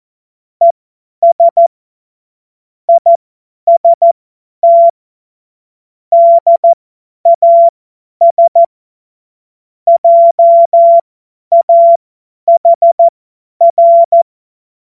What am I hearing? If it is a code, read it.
Morse code, heard as a keyed tone, about 14 wpm, spaced as for 10 wpm, 680 Hz. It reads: ES IST DAS JAHR
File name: Morsekurs / Witze / Witz 29. Juni 2021, 50 F70